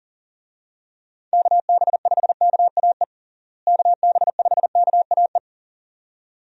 Morse code, heard as a keyed tone, about 40 wpm, 700 Hz. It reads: KB5KAE KB5KAE
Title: USA Call Signs - 6 Characters - Repeated 2x - 40wpm